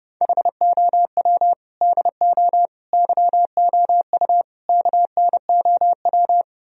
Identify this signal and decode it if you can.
Morse code, heard as a keyed tone, about 36 wpm, 700 Hz. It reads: HOW DO YOU KNOW
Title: Speed-Racing - Sets of 4 Words 30wpm